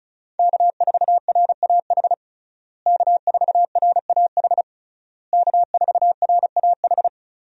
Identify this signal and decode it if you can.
Morse code, heard as a keyed tone, about 36 wpm, 700 Hz. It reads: K4RAH K4RAH K4RAH
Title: USA Call Signs - 5 Characters - Repeated 3x - 35wpm